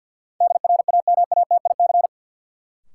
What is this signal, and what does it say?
Morse code, heard as a keyed tone, about 51 wpm, 700 Hz. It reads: DRAMATIC
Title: Sentences from Top 1300 Words - New Word Drill - 50wpm